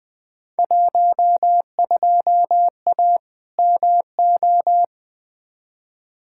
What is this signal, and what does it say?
Morse code, heard as a keyed tone, about 20 wpm, 700 Hz. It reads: 12A MO
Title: Field Day Exchanges Encore - 20wpm